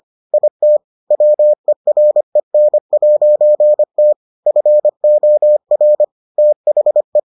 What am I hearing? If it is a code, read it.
Morse code, heard as a keyed tone, about 25 wpm, 600 Hz. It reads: IT WEREN'T FOR THE